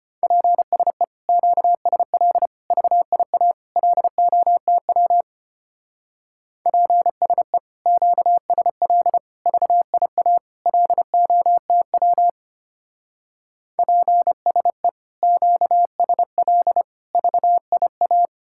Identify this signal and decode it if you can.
Morse code, heard as a keyed tone, about 32 wpm, 700 Hz. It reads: PSE QSL VIA LOTW PSE QSL VIA LOTW PSE QSL VIA